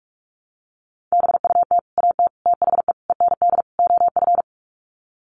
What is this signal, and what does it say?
Morse code, heard as a keyed tone, about 44 wpm, 700 Hz. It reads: BUT AT THE END OF
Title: Sets of 5 Words Encore2 45wpm